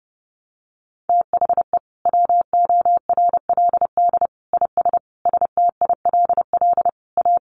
Morse code, heard as a keyed tone, about 30 wpm, 700 Hz. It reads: THE WORLD IS STILL A
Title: Sentences from Top 1200 Words 30wpm